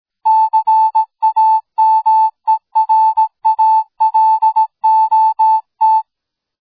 Morse code, heard as a keyed tone, about 17 wpm, 880 Hz. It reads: CAMERALOT